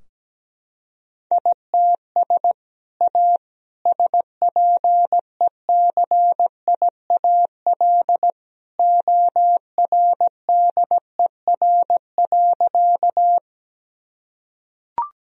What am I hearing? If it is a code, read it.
Morse code, heard as a keyed tone, about 17 wpm, 700 Hz. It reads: ITS A SPECIAL ORDER.
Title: Sentences from Top 1000 Words - Review Part 3 - 17wpm